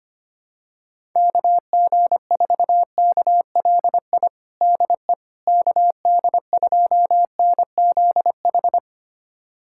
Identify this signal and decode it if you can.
Morse code, heard as a keyed tone, about 25 wpm, 700 Hz. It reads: KG4KLI DE KD2NZH
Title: USA Callsign DE Callsign 25wpm